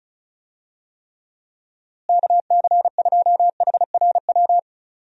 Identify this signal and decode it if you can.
Morse code, heard as a keyed tone, about 34 wpm, 700 Hz. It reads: KC2HRW
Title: USA Call Signs - 6 Characters - Repeated 3x - 35wpm